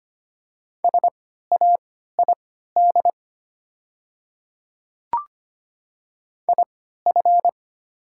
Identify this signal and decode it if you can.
Morse code, heard as a keyed tone, about 25 wpm, spaced as for 14 wpm, 700 Hz. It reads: SAID IF
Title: 3x Character Spacing - Top 100 Words - 25wpm